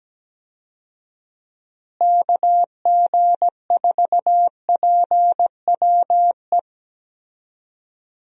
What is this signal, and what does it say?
Morse code, heard as a keyed tone, about 17 wpm, 700 Hz. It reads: KG4PWE
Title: USA Call Signs - 6 Characters - Repeated 3x - 17wpm